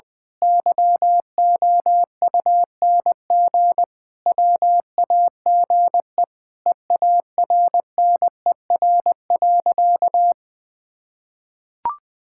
Morse code, heard as a keyed tone, about 20 wpm, 700 Hz. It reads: YOUNG WAGE EARNER.
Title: Pangram Challenge v2 - 20wpm